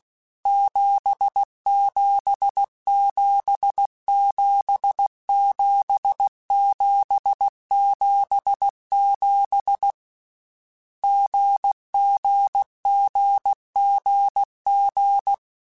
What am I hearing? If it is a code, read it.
Morse code, heard as a keyed tone, about 16 wpm, 800 Hz. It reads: 77777777 GGGGG
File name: Difficult Combos / Difficult Combos-mp3s / GZ GZ7 @15